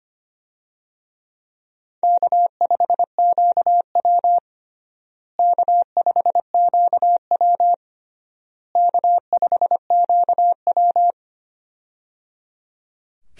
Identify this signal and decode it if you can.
Morse code, heard as a keyed tone, about 25 wpm, 700 Hz. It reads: K5QW K5QW K5QW
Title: USA Call Signs - 4 Characters - Repeated 3x - 25wpm